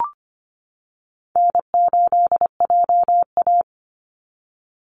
Morse code, heard as a keyed tone, about 25 wpm, 700 Hz. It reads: N8JA
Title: Call Signs - 25wpm